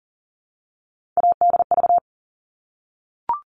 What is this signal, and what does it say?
Morse code, heard as a keyed tone, about 40 wpm, 700 Hz. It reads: ADV
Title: All Cut Numbers - 3-Digit 40wpm